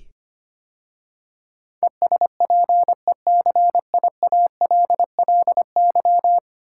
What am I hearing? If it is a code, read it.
Morse code, heard as a keyed tone, about 25 wpm, 690 Hz. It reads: ESPECIALLY
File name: 10 Letter Words - First 3 Letters Given - 25wpm